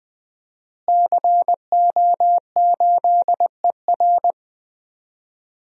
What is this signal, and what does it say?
Morse code, heard as a keyed tone, about 20 wpm, 700 Hz. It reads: CO8ER
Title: DXCC-100 Call Signs - 5 Characters - Repeated 2x - 20wpm